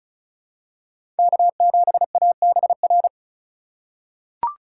Morse code, heard as a keyed tone, about 34 wpm, 700 Hz. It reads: K7ABR
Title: USA Call Signs - 5 Characters - 3x Character Spacing - 35wpm